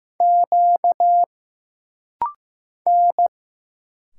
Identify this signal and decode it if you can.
Morse code, heard as a keyed tone, about 15 wpm, 700 Hz. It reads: Q N
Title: Single Letters - Rapid-Fire 15wpm